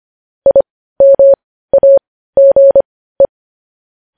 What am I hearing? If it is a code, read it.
Morse code, heard as a keyed tone, about 25 wpm, spaced as for 15 wpm, 550 Hz. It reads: IMAGE